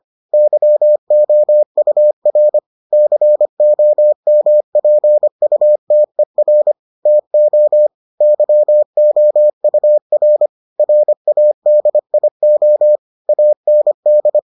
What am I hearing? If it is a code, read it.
Morse code, heard as a keyed tone, about 25 wpm, 600 Hz. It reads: YOUR COMPUTER TO YOUR RADIO AND